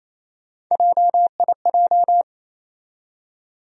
Morse code, heard as a keyed tone, about 28 wpm, 700 Hz. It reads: JIJ
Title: USA Call Sign Postfixes - 3 Letters 28wpm